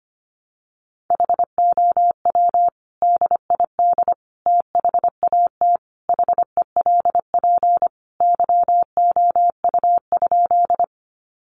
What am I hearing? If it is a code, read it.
Morse code, heard as a keyed tone, about 25 wpm, 700 Hz. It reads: HOW DID THAT HELP YOU?
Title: Sentences from Top 200 Words 25wpm